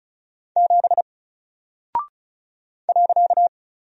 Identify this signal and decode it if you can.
Morse code, heard as a keyed tone, about 36 wpm, 700 Hz. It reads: 7 .
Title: Single Character (Letter:Number:Punctuation:Prosign) Alt 35wpm